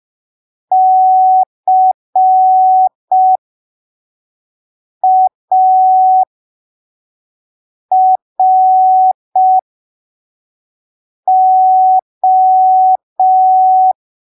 Morse code, heard as a keyed tone, about 5 wpm, spaced as for 3 wpm, 750 Hz. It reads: CARO
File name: arquivos / texto / texto 26 05ppm